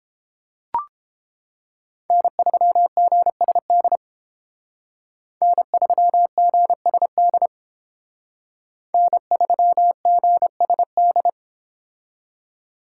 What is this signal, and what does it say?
Morse code, heard as a keyed tone, about 30 wpm, 700 Hz. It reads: N3GSD N3GSD N3GSD